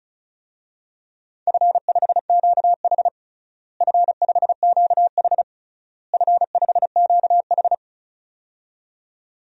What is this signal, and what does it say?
Morse code, heard as a keyed tone, about 36 wpm, 700 Hz. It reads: F5QH F5QH F5QH
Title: DXCC-100 Call Signs - 4 Characters - Repeated 3x - 35wpm